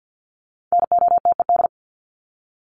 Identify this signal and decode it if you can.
Morse code, heard as a keyed tone, about 49 wpm, 700 Hz. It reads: NOTED
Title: ED Suffix - Top 100 Words with Base Words 50wpm